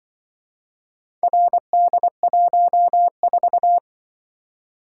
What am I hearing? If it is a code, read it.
Morse code, heard as a keyed tone, about 24 wpm, 700 Hz. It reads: RD14
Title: Speed-Racing - Four Letter-Number Combinations - 20wpm